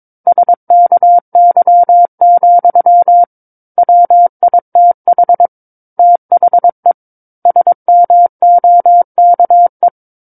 Morse code, heard as a keyed tone, about 22 wpm, 700 Hz. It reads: SKY, WITH THE SMOKE
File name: Book One / 13-Chapter 13 - 22 WPM